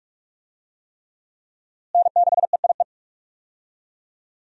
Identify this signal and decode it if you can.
Morse code, heard as a keyed tone, about 46 wpm, 700 Hz. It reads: N6EIE
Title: USA Call Signs - 5 Characters - Repeated 3x - 45wpm